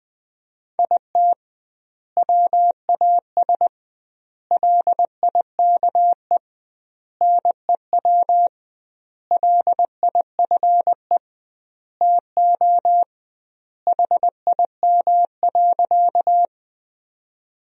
Morse code, heard as a keyed tone, about 20 wpm, 700 Hz. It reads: IT WAS LIKE NEW LIFE TO HIM.
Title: 2x Word-Spacing - Sentences from Top 300 Words - 20wpm